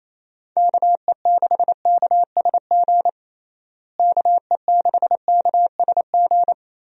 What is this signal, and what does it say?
Morse code, heard as a keyed tone, about 28 wpm, 700 Hz. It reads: KE6KSG KE6KSG
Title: USA Call Signs - 6 Characters - Repeated 2x - 28wpm